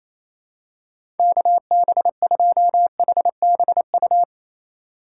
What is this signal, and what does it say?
Morse code, heard as a keyed tone, about 28 wpm, 690 Hz. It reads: KB2HBU
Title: USA Call Signs - 6 Characters - 2x Character Spacing - 28wpm